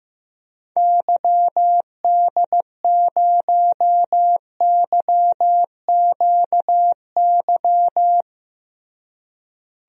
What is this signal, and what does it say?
Morse code, heard as a keyed tone, about 15 wpm, 700 Hz. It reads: YD0YQY
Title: DXCC-100 Call Signs - 6 Characters - Repeated 2x - 15wpm